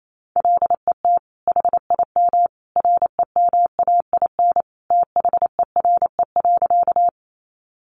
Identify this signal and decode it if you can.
Morse code, heard as a keyed tone, about 28 wpm, 700 Hz. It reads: LET HIM REMAIN THERE.
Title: Sentences from Top 1200 Words 28wpm